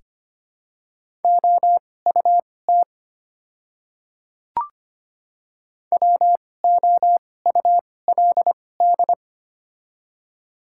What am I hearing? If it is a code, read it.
Morse code, heard as a keyed tone, about 25 wpm, spaced as for 18 wpm, 700 Hz. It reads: OUT WOULD
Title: 2x Character Spacing - Top 100 Words - 25wpm